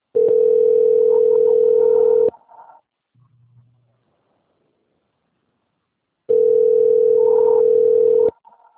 Ringback tone (North America), 440 + 480 Hz.
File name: out- - - - - .56031